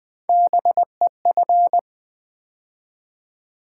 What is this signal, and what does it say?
Morse code, heard as a keyed tone, about 20 wpm, 700 Hz. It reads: BEF
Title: Build A Word - 3 Letters At A Time - 6-Letter Words - 20wpm